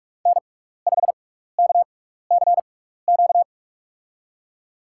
Morse code, heard as a keyed tone, about 41 wpm, spaced as for 15 wpm, 700 Hz. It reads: N5KCQ
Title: USA Call Signs - 5 Characters - 6x Character Spacing - 45wpm